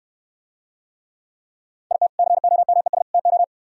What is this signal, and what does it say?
Morse code, heard as a keyed tone, about 65 wpm, 700 Hz. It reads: IT COMES TO